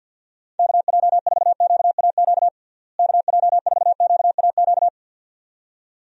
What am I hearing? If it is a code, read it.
Morse code, heard as a keyed tone, about 51 wpm, 700 Hz. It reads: KJ4QAQ KJ4QAQ
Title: USA Call Signs - 6 Characters - Repeated 2x - 50wpm